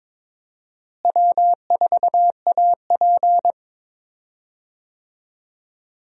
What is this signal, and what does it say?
Morse code, heard as a keyed tone, about 22 wpm, 700 Hz. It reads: W4AP
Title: USA Call Signs - 4 Characters - Repeated 3x - 22wpm